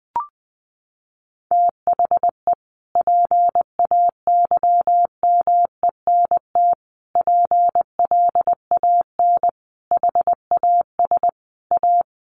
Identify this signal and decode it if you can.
Morse code, heard as a keyed tone, about 20 wpm, 700 Hz. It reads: THE PAYMENT PLAN HAS A
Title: Sentences from Top 1500 Words 20wpm